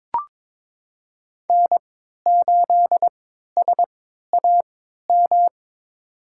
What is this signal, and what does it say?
Morse code, heard as a keyed tone, about 22 wpm, spaced as for 13 wpm, 700 Hz. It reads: N8SAM
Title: USA Call Signs - 5 Characters - 3x Character Spacing - 22wpm